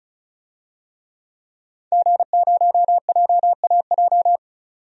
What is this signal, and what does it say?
Morse code, heard as a keyed tone, about 34 wpm, 700 Hz. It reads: G0JAJ
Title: DXCC-100 Call Signs - 5 Characters - 35wpm